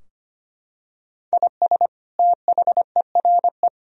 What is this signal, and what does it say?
Morse code, heard as a keyed tone, about 25 wpm, 700 Hz. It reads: IS THERE
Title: Sentences from Top 1000 Words - Review Part 4 - 25wpm